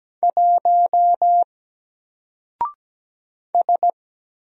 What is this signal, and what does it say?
Morse code, heard as a keyed tone, about 17 wpm, 700 Hz. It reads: 1 S
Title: Single Character (Letter:Number:Punctuation:Prosign) 17wpm